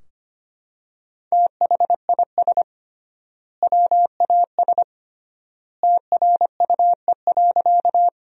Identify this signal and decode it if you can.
Morse code, heard as a keyed tone, about 25 wpm, 700 Hz. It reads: THIS WAS TRUE.
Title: 3x Word-Spacing - Sentences from Top 400 Words - 25wpm